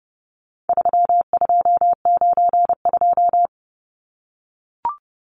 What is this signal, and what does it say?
Morse code, heard as a keyed tone, about 30 wpm, 700 Hz. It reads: 3292
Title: Numbers - 4-Digit 30wpm